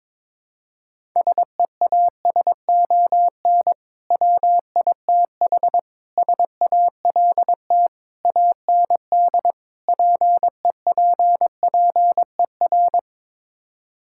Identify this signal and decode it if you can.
Morse code, heard as a keyed tone, about 22 wpm, 700 Hz. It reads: SEASON WITH SALT AND PEPPER
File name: practice / 22wpm / Build A Phrase - 5-Word Phrases - 22wpm